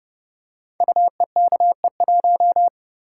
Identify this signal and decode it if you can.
Morse code, heard as a keyed tone, about 30 wpm, 700 Hz. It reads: UEKE1
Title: Code Groups - 5 Letter:Numbers - 30wpm